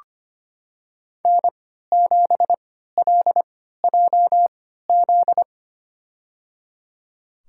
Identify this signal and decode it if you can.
Morse code, heard as a keyed tone, about 25 wpm, spaced as for 14 wpm, 700 Hz. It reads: N7LJZ
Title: USA Call Signs - 3x Character Spacing - 25wpm